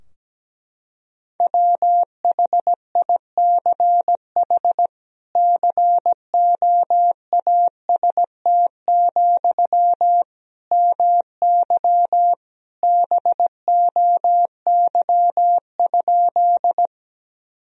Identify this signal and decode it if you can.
Morse code, heard as a keyed tone, about 17 wpm, 700 Hz. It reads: WHICH COAST, MY BOY?
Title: Sentences from Top 1000 Words - Review Part 4 - 17wpm